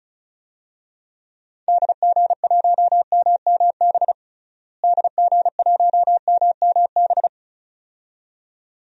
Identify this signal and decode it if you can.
Morse code, heard as a keyed tone, about 36 wpm, 700 Hz. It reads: DG1MMB DG1MMB
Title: DXCC-100 Call Signs - 6 Characters - Repeated 2x - 35wpm